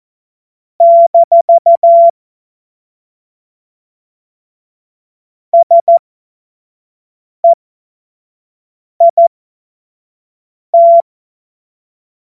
Morse code, heard as a keyed tone, about 14 wpm, spaced as for 5 wpm, 680 Hz. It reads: - SEIT
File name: Witz-23-September-2025-25-F70